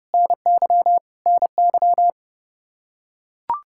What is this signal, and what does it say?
Morse code, heard as a keyed tone, about 32 wpm, 700 Hz. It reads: NY NY
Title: Practice QSO - Exchange 1 - Beginning - 30wpm